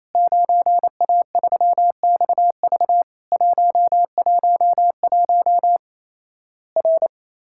Morse code, heard as a keyed tone, about 28 wpm, 700 Hz. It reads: 9A3XV 111 R
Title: Worked All Europe DX Contest - QTC Exchange - 28wpm